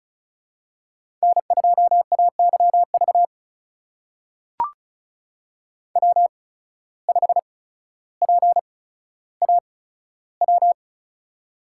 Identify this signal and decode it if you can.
Morse code, heard as a keyed tone, about 36 wpm, spaced as for 15 wpm, 700 Hz. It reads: N2AYV W5PAW
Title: USA Call Signs - 5 Characters - 8x Character Spacing - 35wpm